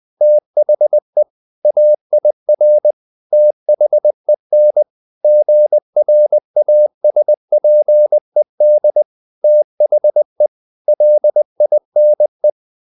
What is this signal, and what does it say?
Morse code, heard as a keyed tone, about 20 wpm, 600 Hz. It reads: THE AIR THEN GRASPED THE LINE
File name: the old man and the sea part 2 20wpm